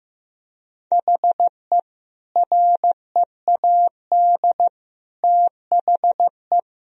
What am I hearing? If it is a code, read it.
Morse code, heard as a keyed tone, about 15 wpm, 700 Hz. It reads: HE READ THE
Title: Sentences from Top 1000 Words - Review Part 1 - 15wpm